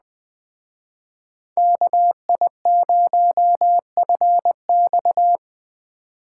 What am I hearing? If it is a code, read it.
Morse code, heard as a keyed tone, about 20 wpm, 700 Hz. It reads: KI0FX
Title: Speed Racing - USA Call Signs 17wpm